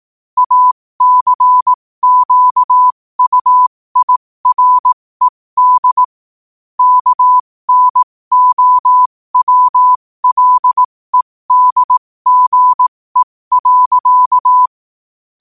Morse code, heard as a keyed tone, about 18 wpm, spaced as for 15 wpm, 1000 Hz. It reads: ACQUIRED KNOWLEDGE.
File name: qotdfiles / 15wpm